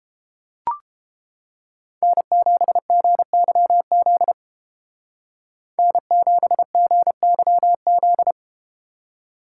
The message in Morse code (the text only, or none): N7GYZ N7GYZ